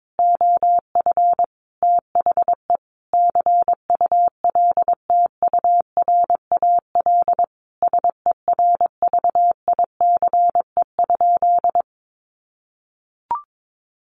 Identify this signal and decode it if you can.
Morse code, heard as a keyed tone, about 22 wpm, 700 Hz. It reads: OF THE CULTURAL SERVICE?
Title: Sentences from Top 1500 Words 22wpm